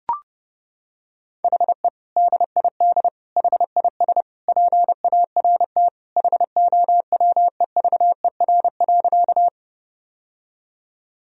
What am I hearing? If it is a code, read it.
Morse code, heard as a keyed tone, about 30 wpm, 700 Hz. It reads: HE DID HIS PART HOWEVER.
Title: Sentences from Top 1000 Words - Review Part 1 - 30wpm